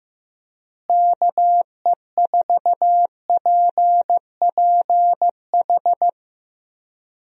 Morse code, heard as a keyed tone, about 15 wpm, 700 Hz. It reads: KE4PPH
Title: USA Call Signs - 6 Characters - Repeated 3x - 15wpm